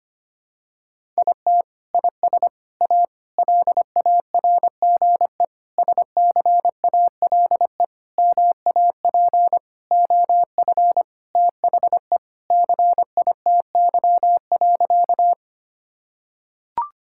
Morse code, heard as a keyed tone, about 25 wpm, 700 Hz. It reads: IT IS A LARGE SCALE MAP OF THE CITY.
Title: Sentences from Top 1000 Words - Review Part 5 - 25wpm